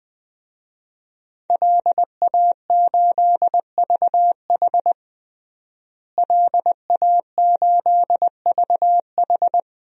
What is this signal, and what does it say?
Morse code, heard as a keyed tone, about 21 wpm, 700 Hz. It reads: LA8VH LA8VH
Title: DXCC-100 Call Signs - 5 Characters - Repeated 2x - 20wpm